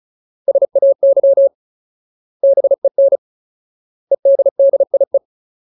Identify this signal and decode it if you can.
Morse code, heard as a keyed tone, about 34 wpm, 550 Hz. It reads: SAY BEN EDDIE
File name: Ultimate Word List / 35WPM Ultimate